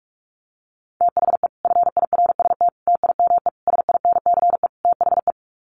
Morse code, heard as a keyed tone, about 44 wpm, 700 Hz. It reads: THE FIRST TIME SINCE THE
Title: Sets of 5 Words Encore 45wpm